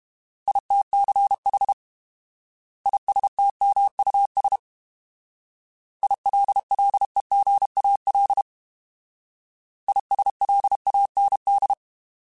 Morse code, heard as a keyed tone, about 31 wpm, 800 Hz. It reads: ITCH ISTMUS ILLEGAL ISLAND